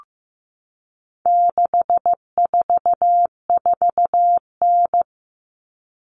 Morse code, heard as a keyed tone, about 15 wpm, 700 Hz. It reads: 644N
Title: All Cut Numbers - 4-Digit 15wpm